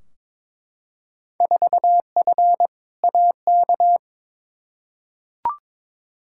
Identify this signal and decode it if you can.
Morse code, heard as a keyed tone, about 22 wpm, 700 Hz. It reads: 4F AK
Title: Field Day Exchanges Encore - 22wpm